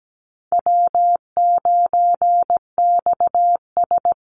Morse code, heard as a keyed tone, about 17 wpm, 700 Hz. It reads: W9XS
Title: Call Signs Repeated 3x - 17wpm